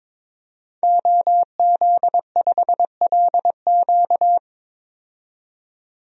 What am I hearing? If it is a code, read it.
Morse code, heard as a keyed tone, about 22 wpm, 700 Hz. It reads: OZ5LQ
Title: DXCC-100 Call Signs - 5 Characters - 22wpm